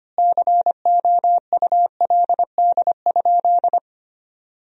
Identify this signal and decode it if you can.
Morse code, heard as a keyed tone, about 25 wpm, 700 Hz. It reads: COULD?
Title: 4x Word-Spacing - Sentences from Top 400 Words - 25wpm